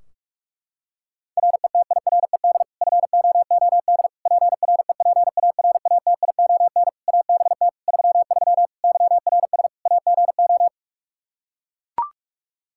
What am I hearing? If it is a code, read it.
Morse code, heard as a keyed tone, about 41 wpm, 700 Hz. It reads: RETIRED FOOD PREPARATION ABT 33 YRS AGO
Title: K1USN SST QSO Practice 45wpm